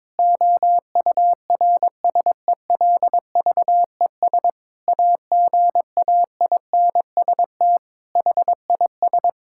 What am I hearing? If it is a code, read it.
Morse code, heard as a keyed tone, about 22 wpm, 700 Hz. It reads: OURSELVES AGAINST HIS
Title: Sentences from Top 1600 Words 22wpm